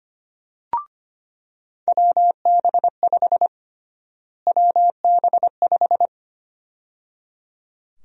Morse code, heard as a keyed tone, about 25 wpm, 700 Hz. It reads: WB5 WB5
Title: USA Call Sign Prefixes - Repeated 2x - 25wpm